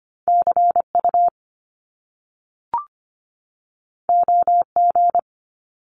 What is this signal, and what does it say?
Morse code, heard as a keyed tone, about 25 wpm, 700 Hz. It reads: CU OG
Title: Top Two Letter Combinations in Words 25wpm